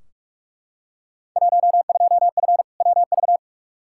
Morse code, heard as a keyed tone, about 44 wpm, 700 Hz. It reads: 12F WV
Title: Field Day Exchanges Encore - 45wpm